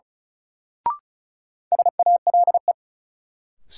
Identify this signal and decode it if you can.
Morse code, heard as a keyed tone, about 34 wpm, 700 Hz. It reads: SALE
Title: Sentences from Top 1400 Words - New Word Drill - 35wpm